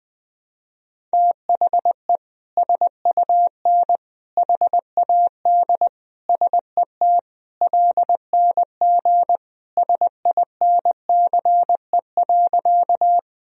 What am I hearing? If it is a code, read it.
Morse code, heard as a keyed tone, about 21 wpm, 700 Hz. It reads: THE SUN HAD SET LNG SINCE.
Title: Sentences from Top 300 Words and Abbreviations 20wpm